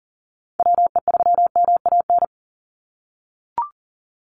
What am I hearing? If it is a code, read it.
Morse code, heard as a keyed tone, about 39 wpm, 690 Hz. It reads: WE3MAN
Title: USA Call Signs 40wpm